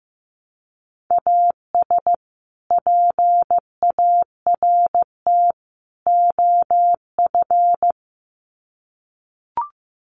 Morse code, heard as a keyed tone, about 15 wpm, 700 Hz. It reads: AS PART OF
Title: Sets of 3 Words 15wpm